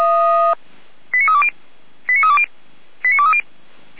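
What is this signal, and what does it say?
Five-tone selective call (ZVEI), heard as quick runs of five tones, tones of about 70 ms. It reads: 89320 89320 89230
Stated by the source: Received on an SDR, ZVEI five-tone selective call.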